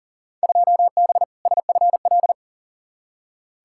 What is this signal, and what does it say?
Morse code, heard as a keyed tone, about 40 wpm, 700 Hz. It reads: 2B SFL
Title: Field Day Exchanges - 40wpm